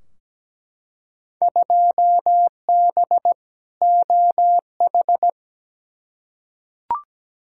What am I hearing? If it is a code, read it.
Morse code, heard as a keyed tone, about 17 wpm, 700 Hz. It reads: 2B OH
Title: Field Day Exchanges - 17wpm